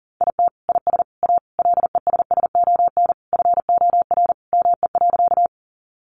Morse code, heard as a keyed tone, about 40 wpm, 700 Hz. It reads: IT IS A LESSON FOR ME.
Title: Sentences from Top 1200 Words 40wpm